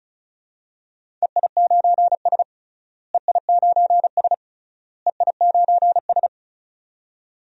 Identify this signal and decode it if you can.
Morse code, heard as a keyed tone, about 34 wpm, 700 Hz. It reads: EI9S EI9S EI9S
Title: DXCC-100 Call Signs - 4 Characters - Repeated 3x - 35wpm